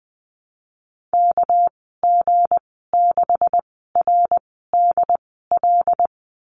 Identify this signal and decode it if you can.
Morse code, heard as a keyed tone, about 20 wpm, spaced as for 15 wpm, 700 Hz. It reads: KG6RDL
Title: 200 Lesson 32g - Letter k - Call Signs 20wpm